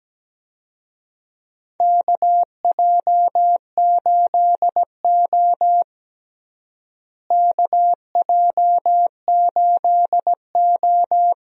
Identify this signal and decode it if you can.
Morse code, heard as a keyed tone, about 17 wpm, 700 Hz. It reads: KJ8O KJ8O
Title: USA Call Signs - 4 Characters - Repeated 2x - 17wpm